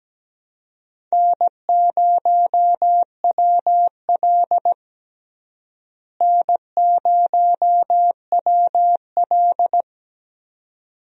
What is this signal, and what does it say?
Morse code, heard as a keyed tone, about 17 wpm, 700 Hz. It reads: N0WL N0WL
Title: USA Call Signs - 4 Characters - Repeated 2x - 17wpm